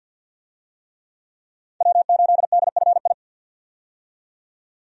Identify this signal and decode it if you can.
Morse code, heard as a keyed tone, about 53 wpm, 700 Hz. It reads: W8DFI
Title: USA Call Signs - 5 Characters - 50wpm